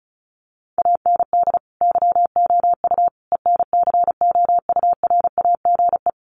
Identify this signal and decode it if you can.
Morse code, heard as a keyed tone, about 34 wpm, 700 Hz. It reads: AND YOU ENCOURAGE
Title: Sentences from Top 1400 Words 35wpm